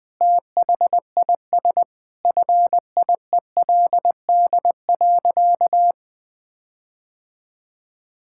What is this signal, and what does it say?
Morse code, heard as a keyed tone, about 21 wpm, 700 Hz. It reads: THIS FIELD.
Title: Sentences with 5-Word Phrases - 20wpm